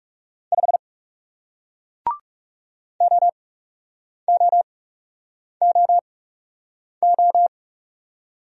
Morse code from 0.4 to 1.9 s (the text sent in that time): 5